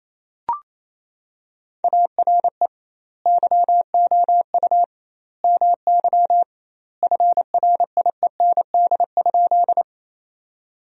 Morse code, heard as a keyed tone, about 28 wpm, 700 Hz. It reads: ARE YOU MY FRIEND?
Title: 2x Word-Spacing - Sentences from Top 300 Words - 28wpm